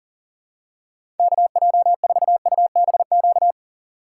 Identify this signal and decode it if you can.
Morse code, heard as a keyed tone, about 40 wpm, 700 Hz. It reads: KJ4UBQ
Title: USA Call Signs - 6x Character Spacing - 40wpm